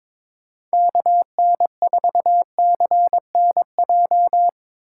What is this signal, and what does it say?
Morse code, heard as a keyed tone, about 22 wpm, 700 Hz. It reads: KN4CNJ
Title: USA Call Signs - 6 Characters - 2x Character Spacing - 22wpm